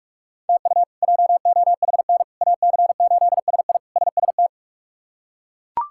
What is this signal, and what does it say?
Morse code, heard as a keyed tone, about 42 wpm, 700 Hz. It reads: TU JOHN AC8SI SST
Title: K1USN SST QSO Practice - 45wpm